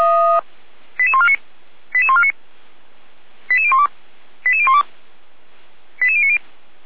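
Five-tone selective call (ZVEI), heard as quick runs of five tones, tones of about 80 ms. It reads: 88149 88149 88012 88012 88090
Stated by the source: ZVEI five-tone selective call received by SDR, from feuerwehr-orlamuende.org fire brigade audio archive.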